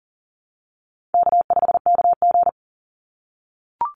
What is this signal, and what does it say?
Morse code, heard as a keyed tone, about 40 wpm, 700 Hz. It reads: K5KG
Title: Easy Call Signs - 40wpm